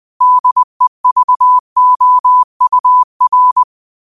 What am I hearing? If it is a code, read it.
Morse code, heard as a keyed tone, about 20 wpm, 1000 Hz. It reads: DEVOUR